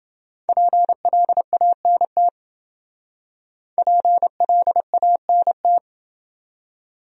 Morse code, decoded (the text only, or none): PLANT PLANT